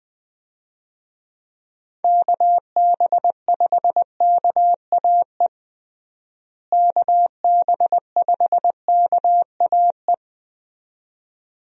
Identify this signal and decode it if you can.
Morse code, heard as a keyed tone, about 21 wpm, 700 Hz. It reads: KB5KAE KB5KAE
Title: USA Call Signs - 6 Characters - Repeated 2x - 20wpm